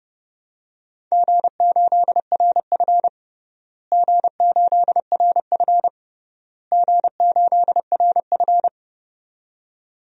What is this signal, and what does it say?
Morse code, heard as a keyed tone, about 29 wpm, 700 Hz. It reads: G8RF G8RF G8RF
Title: DXCC-100 Call Signs - 4 Characters - Repeated 3x - 30wpm